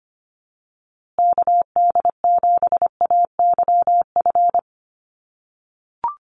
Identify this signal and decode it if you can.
Morse code, heard as a keyed tone, about 25 wpm, 700 Hz. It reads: KD7AYF
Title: USA Call Signs Repeated 3x 25wpm